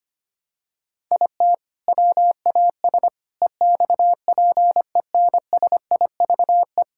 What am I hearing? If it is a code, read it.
Morse code, heard as a keyed tone, about 25 wpm, 700 Hz. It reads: IT WAS EXPENSIVE
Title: Sentences from Top 1000 Words - Review Part 6 - 25wpm